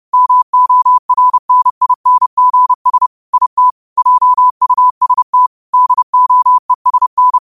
Morse code, heard as a keyed tone, about 30 wpm, 1000 Hz. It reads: MORNINGS IT JUST DOESN
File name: qotdfiles / 30wpm